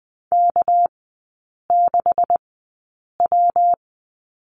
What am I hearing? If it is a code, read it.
Morse code, heard as a keyed tone, about 20 wpm, spaced as for 8 wpm, 690 Hz. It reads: K6W
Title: 198 Lesson 32e - Letter k - Three Characters 20wpm